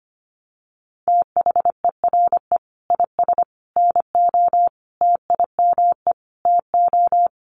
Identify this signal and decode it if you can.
Morse code, heard as a keyed tone, about 25 wpm, 700 Hz. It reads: THERE IS NO TIME TO
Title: Sentences from Top 1100 Words 25wpm